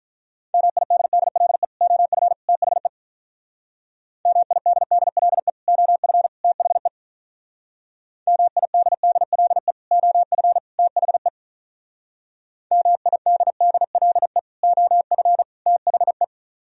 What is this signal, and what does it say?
Morse code, heard as a keyed tone, about 43 wpm, 700 Hz. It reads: MIDDLE OF THE MIDDLE OF THE MIDDLE OF THE MIDDLE OF THE
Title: Speed-Racing - Sets of 3 Words 35wpm